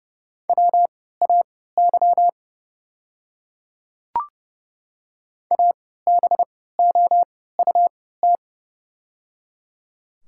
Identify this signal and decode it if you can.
Morse code, heard as a keyed tone, about 30 wpm, spaced as for 17 wpm, 700 Hz. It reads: WAY ABOUT
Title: 3x Character Spacing - Top 200 Words - 30wpm